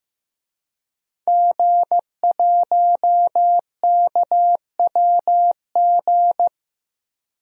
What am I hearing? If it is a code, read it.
Morse code, heard as a keyed tone, about 16 wpm, 700 Hz. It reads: G1KWG
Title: DXCC-100 Call Signs - 5 Characters - Repeated 2x - 15wpm